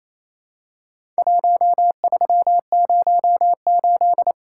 Morse code, heard as a keyed tone, about 28 wpm, 700 Hz. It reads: 1308
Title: Worked All Europe DX Contest - QTC Exchange - 28wpm